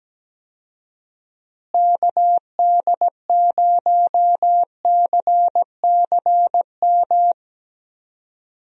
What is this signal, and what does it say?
Morse code, heard as a keyed tone, about 17 wpm, 700 Hz. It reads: KD0CCM
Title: USA Call Signs - 6 Characters - Repeated 2x - 17wpm